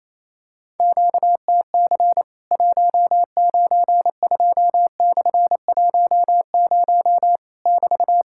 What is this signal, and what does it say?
Morse code, heard as a keyed tone, about 28 wpm, 700 Hz. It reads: QTC 192/10 <BT>
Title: Worked All Europe DX Contest - QTC Exchange - 28wpm